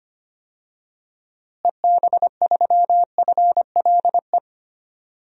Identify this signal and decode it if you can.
Morse code, heard as a keyed tone, about 25 wpm, 700 Hz. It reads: EB3FLE
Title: DXCC-100 Call Signs - 6 Characters - Repeated 2x - 25wpm